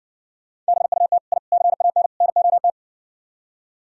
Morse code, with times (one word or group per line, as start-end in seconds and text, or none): 0.7-1.2 BUT
1.3-1.4 I
1.5-2.1 CAN
2.2-2.7 NOT